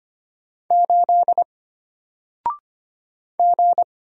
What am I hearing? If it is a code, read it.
Morse code, heard as a keyed tone, about 25 wpm, 700 Hz. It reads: 8 G
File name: Single-Letter-Number-25wpm